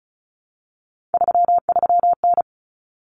Morse code, heard as a keyed tone, about 34 wpm, 700 Hz. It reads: 33N
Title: Common Cut Numbers - 3-Digit 35wpm